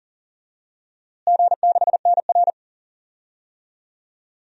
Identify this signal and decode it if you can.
Morse code, heard as a keyed tone, about 40 wpm, 700 Hz. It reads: G6NR
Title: DXCC-100 Call Signs - 4 Characters - Repeated 3x - 40wpm